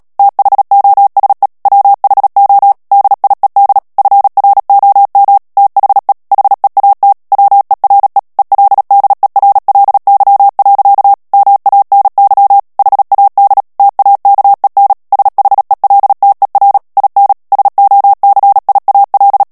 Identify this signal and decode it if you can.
Morse code, heard as a keyed tone, about 37 wpm, 800 Hz. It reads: THOSE WHO DIED FROM THE HEAT WERE ELDERLY. MANY HAD TAKEN SHELTER IN SOCIAL
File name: morse mp3 / ANGLAIS texte clair 2 vitesse 35 mots minute